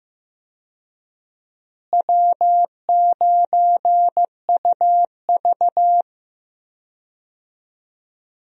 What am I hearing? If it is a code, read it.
Morse code, heard as a keyed tone, about 15 wpm, 700 Hz. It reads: W9UV
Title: USA Call Signs - 4 Characters - Repeated 2x - 15wpm